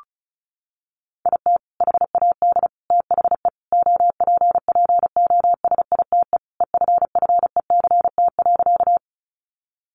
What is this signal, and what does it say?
Morse code, heard as a keyed tone, about 34 wpm, 700 Hz. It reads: IT HAD THE OPPOSITE EFFECT.
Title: Sentences from Top 1300 Words 35wpm